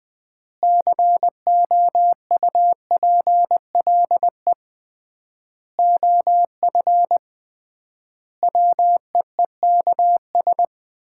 Morse code, heard as a keyed tone, about 21 wpm, 700 Hz. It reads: COUPLE OF WEEKS
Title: 3x Word-Spacing - Sets of 5 Words - 20wpm